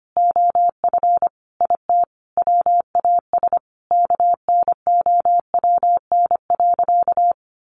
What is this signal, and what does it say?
Morse code, heard as a keyed tone, about 25 wpm, 700 Hz. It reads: OF IT WAS KNOWN.
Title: Sentences from Top 1400 Words 25wpm